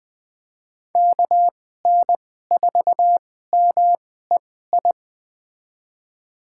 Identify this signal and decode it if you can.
Morse code, heard as a keyed tone, about 20 wpm, spaced as for 14 wpm, 700 Hz. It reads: KN4MEI
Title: USA Call Signs - 2x Character Spacing - 20wpm